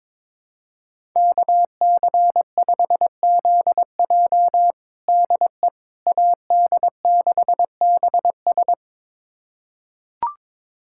Morse code, heard as a keyed tone, about 22 wpm, 700 Hz. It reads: KC5ZJ DE AD6BS
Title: USA Callsign DE Callsign - 5 Characters - 22wpm